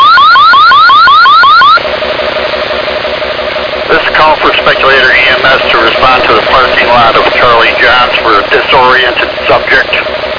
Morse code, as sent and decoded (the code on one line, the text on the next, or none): none